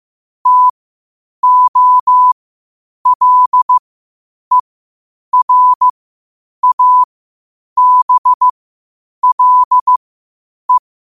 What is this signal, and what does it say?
Morse code, heard as a keyed tone, about 15 wpm, spaced as for 8 wpm, 1000 Hz. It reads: TOLERABLE
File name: qotdfiles / 8wpm